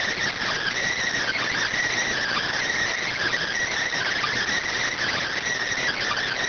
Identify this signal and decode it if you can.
Emergency Alert System (SAME) tones, heard as harsh buzzing bursts. It is heard from 1.2 to 5.4 s.